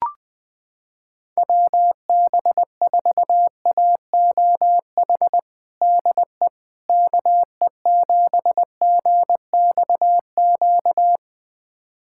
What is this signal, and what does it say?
Morse code, heard as a keyed tone, about 20 wpm, 700 Hz. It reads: WB4AOH DE KE7GXQ
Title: USA Callsign DE Callsign 20wpm